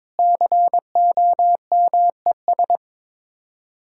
Morse code, heard as a keyed tone, about 22 wpm, 700 Hz. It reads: COMES
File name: practice / 22wpm / Build A Phrase - 5-Word Phrases - 22wpm